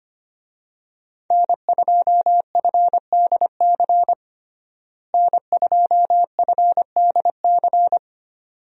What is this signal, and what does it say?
Morse code, heard as a keyed tone, about 25 wpm, 700 Hz. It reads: N2FDC N2FDC
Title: USA Call Signs - 5 Characters - Repeated 2x - 25wpm